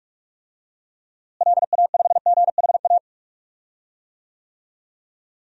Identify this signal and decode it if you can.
Morse code, heard as a keyed tone, about 43 wpm, 700 Hz. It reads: LA5GHA
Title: DXCC-100 Call Signs - 6 Characters - Repeated 3x - 45wpm